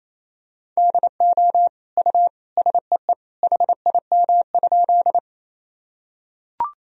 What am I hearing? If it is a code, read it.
Morse code, heard as a keyed tone, about 28 wpm, 700 Hz. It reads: DO U SEE HIM?
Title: Sentences from Top 100 Words and Abbreviations 28wpm